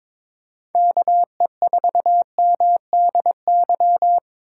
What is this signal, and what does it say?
Morse code, heard as a keyed tone, about 22 wpm, 700 Hz. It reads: KE4MDY
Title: USA Call Signs - 3x Character Spacing - 22wpm